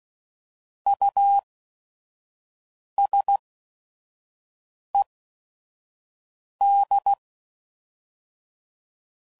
Morse code, heard as a keyed tone, about 16 wpm, spaced as for 5 wpm, 800 Hz. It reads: USED
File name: CW Code Groups @5WPM / Technical Subjects / Reactance